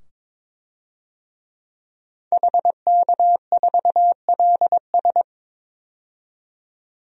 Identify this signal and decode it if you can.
Morse code, heard as a keyed tone, about 22 wpm, 700 Hz. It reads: HK4LS
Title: DXCC-100 Call Signs - 5 Characters - 22wpm